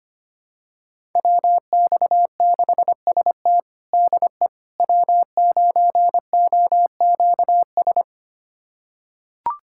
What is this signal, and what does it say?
Morse code, heard as a keyed tone, about 25 wpm, 700 Hz. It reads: WX6ST DE W9OQS
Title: USA Callsign DE Callsign - 5 Characters - 25wpm